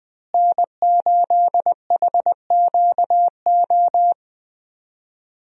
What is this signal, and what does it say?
Morse code, heard as a keyed tone, about 20 wpm, 700 Hz. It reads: N8HQO
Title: USA Call Signs - 5 Characters - Repeated 3x - 20wpm